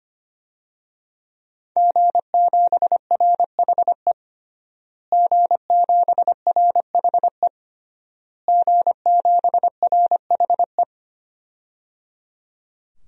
Morse code, heard as a keyed tone, about 25 wpm, 700 Hz. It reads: G7RHE G7RHE G7RHE
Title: DXCC-100 Call Signs - 5 Characters - Repeated 3x - 25wpm